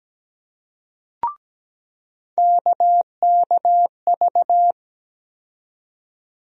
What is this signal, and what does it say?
Morse code, heard as a keyed tone, about 17 wpm, 700 Hz. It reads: KKV
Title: USA Call Sign Postfixes - 3 Letters 17wpm